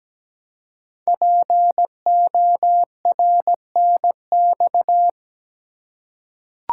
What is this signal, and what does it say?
Morse code, heard as a keyed tone, about 17 wpm, 700 Hz. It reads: PORNX
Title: Code Groups - 5 Letters - 17wpm